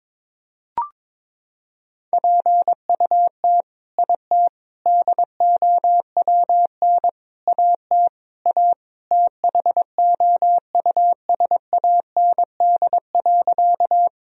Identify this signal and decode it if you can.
Morse code, heard as a keyed tone, about 21 wpm, 700 Hz. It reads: PUT IT DOWN AT A THOUSAND.
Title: Sentences from Top 1000 Words - Review Part 3 - 22wpm